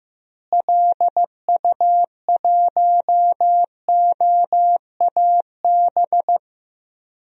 Morse code, heard as a keyed tone, about 15 wpm, 700 Hz. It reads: LU1OAB
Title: DXCC-100 Call Signs - 6 Characters - Repeated 3x - 15wpm